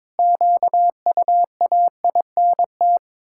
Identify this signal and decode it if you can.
Morse code, heard as a keyed tone, about 22 wpm, 700 Hz. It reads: QUAINT
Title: Pangram Challenge v2 - 22wpm